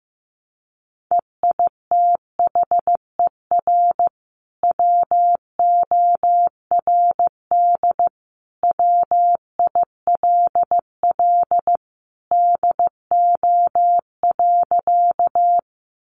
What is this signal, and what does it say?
Morse code, heard as a keyed tone, about 15 wpm, 700 Hz. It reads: EITHER WORD WILL DO.
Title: Sentences from Top 600 Words 15wpm